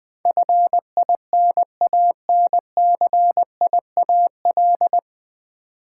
Morse code, heard as a keyed tone, about 20 wpm, 700 Hz. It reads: FINANCIAL
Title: Nine Letter Words - 20wpm